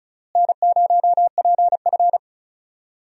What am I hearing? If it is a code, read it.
Morse code, heard as a keyed tone, about 36 wpm, 700 Hz. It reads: N0PF
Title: USA Call Signs - 4 Characters - Repeated 2x - 35wpm